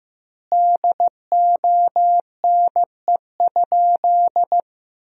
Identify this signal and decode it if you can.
Morse code, heard as a keyed tone, about 15 wpm, 700 Hz. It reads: DONE?
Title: 4x Word-Spacing - Sentences from Top 500 Words - 15wpm